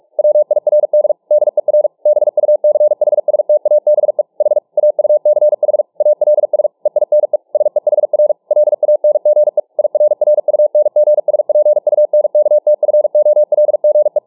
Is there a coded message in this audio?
Yes: Morse code from 0.2 to 14.2 s, heard as a keyed tone, about 43 wpm, 600 Hz. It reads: WIRD DER BUCHSTABE H AUCH ALS EINE SEHR LANGE IRRUNGSPUNKTFOLGE